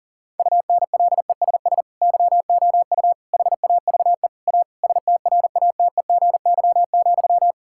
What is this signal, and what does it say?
Morse code, heard as a keyed tone, about 40 wpm, 700 Hz. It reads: UNLESS YOU HAVE A STRATEGY,
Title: Sentences from Top 1000 Words - Review Part 5 - 40wpm